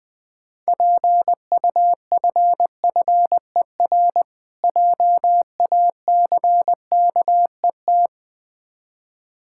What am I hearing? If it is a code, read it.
Morse code, heard as a keyed tone, about 20 wpm, 690 Hz. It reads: PUFFER JACKET
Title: Difficult Words with Hints - 20wpm